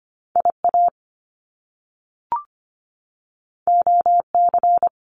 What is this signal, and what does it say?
Morse code, heard as a keyed tone, about 25 wpm, 700 Hz. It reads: I A O C
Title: Top Two Letter Combinations in Words 25wpm